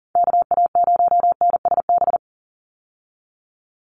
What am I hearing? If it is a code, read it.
Morse code, heard as a keyed tone, about 40 wpm, 700 Hz. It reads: KA0NSB
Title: USA Call Signs 40wpm